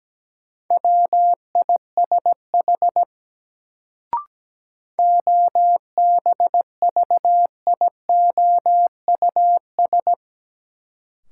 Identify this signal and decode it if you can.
Morse code, heard as a keyed tone, about 17 wpm, 700 Hz. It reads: WISH OBVIOUS
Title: Sentences from Top 1600 Words - New Word Drill - 17wpm